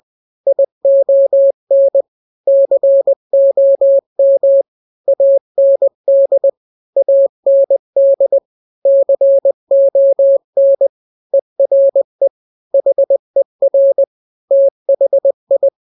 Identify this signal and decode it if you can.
Morse code, heard as a keyed tone, about 20 wpm, 550 Hz. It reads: ION COM AND AND CON ERE HER THI
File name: n-grams / 3-grams 20 18 WPM